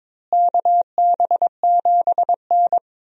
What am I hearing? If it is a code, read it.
Morse code, heard as a keyed tone, about 22 wpm, 700 Hz. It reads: KB7N
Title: USA Call Signs - 4 Characters - Repeated 3x - 22wpm